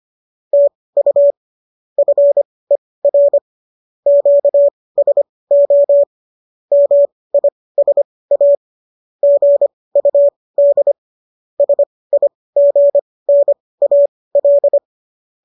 Morse code, heard as a keyed tone, about 25 wpm, spaced as for 18 wpm, 570 Hz. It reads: TU FER QSO MISA GUD SIGNAL